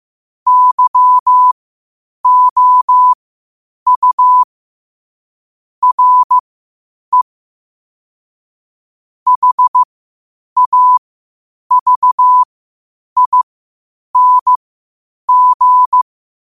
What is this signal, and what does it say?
Morse code, heard as a keyed tone, about 15 wpm, spaced as for 8 wpm, 1000 Hz. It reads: YOU RE HAVING
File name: qotdfiles / 8wpm